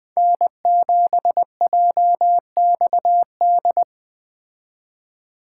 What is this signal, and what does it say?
Morse code, heard as a keyed tone, about 20 wpm, 700 Hz. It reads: N7JXD
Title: Speed Racing - USA 1x3 Call Signs 17wpm